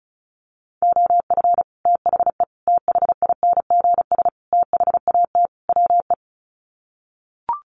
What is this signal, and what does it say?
Morse code, heard as a keyed tone, about 34 wpm, 700 Hz. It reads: OF THE THINGS THAT WE
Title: Sets of 5 Words Encore 35wpm